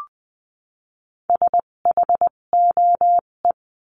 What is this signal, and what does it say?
Morse code, heard as a keyed tone, about 20 wpm, spaced as for 17 wpm, 700 Hz. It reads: SHOE
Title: 257 Lesson 42b - Words 14-20wpm